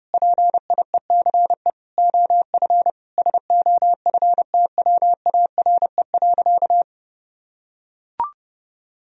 Morse code, heard as a keyed tone, about 30 wpm, 700 Hz. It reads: PIECE OF SOFTWARE.
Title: Sentences from Top 1000 Words - Review Part 3 - 30wpm